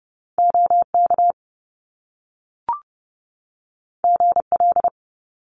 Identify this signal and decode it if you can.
Morse code, heard as a keyed tone, about 30 wpm, 700 Hz. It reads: OK GL
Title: Top Two Letter Combinations in Words 30wpm